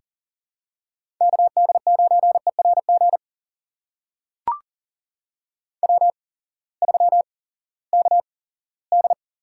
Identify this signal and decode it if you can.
Morse code, heard as a keyed tone, about 40 wpm, spaced as for 20 wpm, 700 Hz. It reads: KD9ERG W3KD
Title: USA Call Signs - 8x Character Spacing - 40wpm